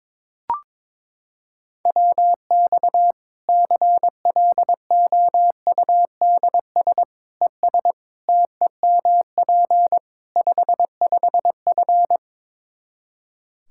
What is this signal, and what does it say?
Morse code, heard as a keyed tone, about 22 wpm, 700 Hz. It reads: WX CLOUDS ES TEMP 55F
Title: K1USN SST QSO Practice 22wpm